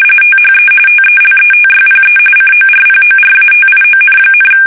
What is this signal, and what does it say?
RTTY, about 45.45 baud. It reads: THE TELETYPE CORPORATION,
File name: fsk850 / teletype-corp